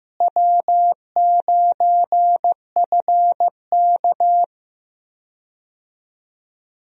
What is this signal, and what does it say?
Morse code, heard as a keyed tone, about 15 wpm, 700 Hz. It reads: W9FK
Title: USA Call Signs - 4 Characters - Repeated 2x - 15wpm